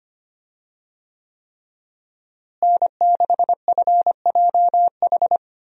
Morse code, heard as a keyed tone, about 25 wpm, 700 Hz. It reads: N6FJH
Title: USA Call Signs - 5 Characters - Repeated 3x - 25wpm